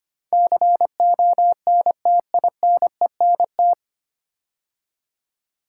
Morse code, heard as a keyed tone, about 25 wpm, 690 Hz. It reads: CONTINENT
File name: practice / 25wpm / 9 Letter Words - First 3 Letters Given - 25wpm